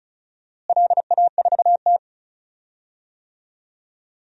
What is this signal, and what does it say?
Morse code, heard as a keyed tone, about 36 wpm, 700 Hz. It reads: LA4T
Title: DXCC-100 Call Signs - 4 Characters - Repeated 3x - 35wpm